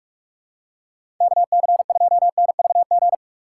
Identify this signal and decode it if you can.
Morse code, heard as a keyed tone, about 46 wpm, 700 Hz. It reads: KC2NVG
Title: USA Call Signs - 6 Characters - 3x Character Spacing - 45wpm